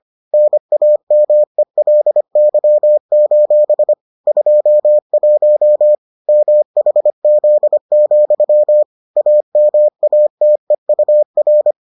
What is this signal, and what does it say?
Morse code, heard as a keyed tone, about 25 wpm, 600 Hz. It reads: NAMELY: 21 MHZ, AMATEUR